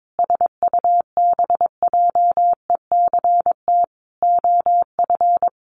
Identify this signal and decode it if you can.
Morse code, heard as a keyed tone, about 22 wpm, 700 Hz. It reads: SUBJECT OF
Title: Sentences from Top 900 Words 22wpm